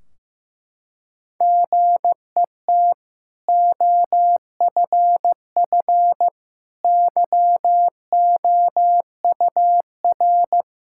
Morse code, heard as a keyed tone, about 16 wpm, 700 Hz. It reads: GET OFF YOUR
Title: Sentences from Top 1600 Words 15wpm